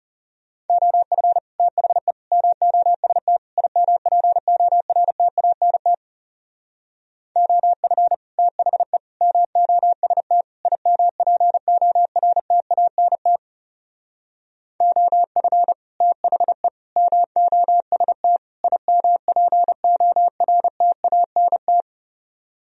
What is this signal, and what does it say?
Morse code, heard as a keyed tone, about 36 wpm, 700 Hz. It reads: OF THE MOST IMPORTANT OF THE MOST IMPORTANT OF THE MOST IMPORTANT
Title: Speed-Racing - Sets of 4 Words 30wpm